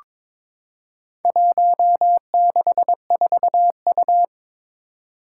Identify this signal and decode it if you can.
Morse code, heard as a keyed tone, about 22 wpm, 700 Hz. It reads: 164U
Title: Four Challenging Letter-Number Combinations 22wpm